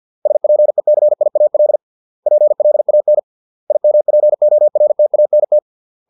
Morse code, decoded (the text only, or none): SPECIAL PLAN IMPORTANT